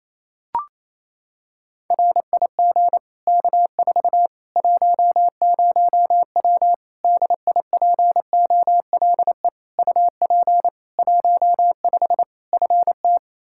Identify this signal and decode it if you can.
Morse code, heard as a keyed tone, about 28 wpm, 700 Hz. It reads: RIG K4 10W DIPOLE UP 15 FT
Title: Practice QSO - Exchange 2 - Info - 28wpm